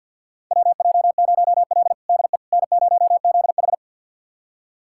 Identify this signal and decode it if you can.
Morse code, heard as a keyed tone, about 47 wpm, 700 Hz. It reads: WJ0L DE N0ZH
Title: USA Callsign DE Callsign - 4 Characters - 50wpm